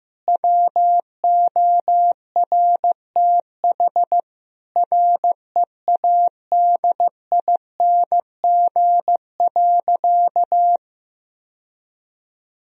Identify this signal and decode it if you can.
Morse code, heard as a keyed tone, about 15 wpm, 700 Hz. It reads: WORTH READING.
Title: Sentences from Top 1000 Words - Review Part 3 - 15wpm